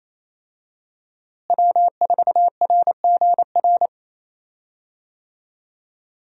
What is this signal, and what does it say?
Morse code, heard as a keyed tone, about 28 wpm, 700 Hz. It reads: W4RGR
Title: USA Call Signs - 5 Characters - Repeated 3x - 28wpm